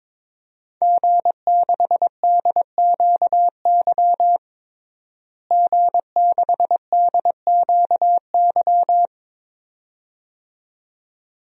Morse code, heard as a keyed tone, about 21 wpm, 700 Hz. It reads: G6DQY G6DQY
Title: DXCC-100 Call Signs - 5 Characters - Repeated 2x - 22wpm